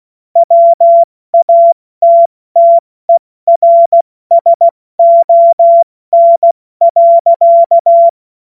Morse code, heard as a keyed tone, about 16 wpm, spaced as for 14 wpm, 680 Hz. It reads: WATTERSON.
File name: Morsekurs / Witze / Witz 23 Januar 2024 70 F80